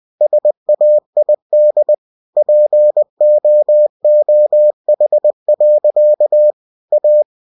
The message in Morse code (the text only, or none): SAID POOH. A